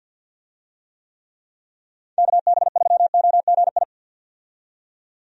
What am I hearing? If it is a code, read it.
Morse code, heard as a keyed tone, about 52 wpm, 700 Hz. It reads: KB3OGI
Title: USA Call Signs - 6 Characters - 50wpm